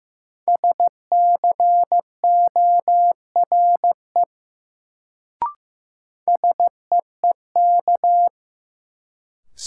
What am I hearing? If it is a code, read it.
Morse code, heard as a keyed tone, about 15 wpm, 700 Hz. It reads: SCORE SEEK
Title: Sentences from Top 1600 Words - New Word Drill Alt - 15wpm